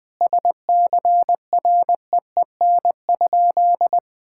Morse code, heard as a keyed tone, about 20 wpm, 700 Hz. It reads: SCREEN?
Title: Sentences from Top 1000 Words - Review Part 6 - 20wpm